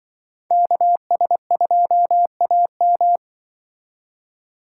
Morse code, heard as a keyed tone, about 25 wpm, 700 Hz. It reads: KS2AM
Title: Speed Racing - USA Call Signs 20wpm